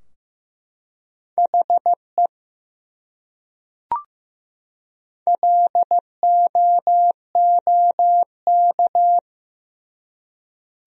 Morse code, heard as a keyed tone, about 15 wpm, 700 Hz. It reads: HE LOOK
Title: Top 100 Words 15wpm